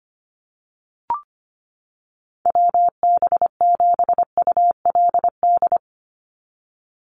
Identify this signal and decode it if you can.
Morse code, heard as a keyed tone, about 25 wpm, 700 Hz. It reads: WB7ULD
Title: NA SOTA Chaser Callsigns 25wpm